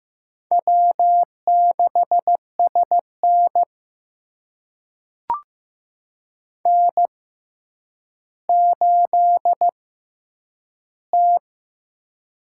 Morse code, heard as a keyed tone, about 15 wpm, spaced as for 9 wpm, 700 Hz. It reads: W6SN N8T
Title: USA Call Signs - 6x Character Spacing - 15wpm